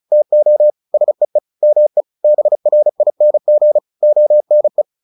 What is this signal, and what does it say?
Morse code, heard as a keyed tone, about 34 wpm, 600 Hz. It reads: TO SEE ME BRING ONE
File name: the old man and the sea / the old man and the sea 1 35wpm